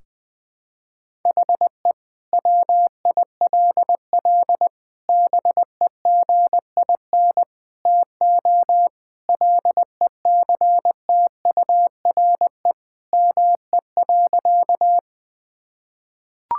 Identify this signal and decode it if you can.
Morse code, heard as a keyed tone, about 20 wpm, 700 Hz. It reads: HE WILL BEGIN TO LECTURE ME.
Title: Sentences from Top 1600 Words 20wpm